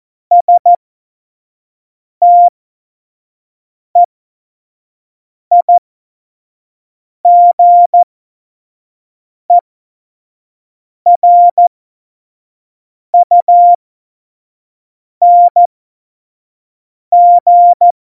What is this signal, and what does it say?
Morse code, heard as a keyed tone, about 14 wpm, spaced as for 5 wpm, 700 Hz. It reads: STEIGERUNG
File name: Morsekurs / Witze / Witz 07. Mai 2019, 25 F70